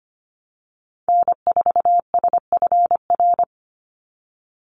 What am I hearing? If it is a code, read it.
Morse code, heard as a keyed tone, about 25 wpm, 700 Hz. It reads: N4SFR
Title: NA SOTA Activator Callsigns 25wpm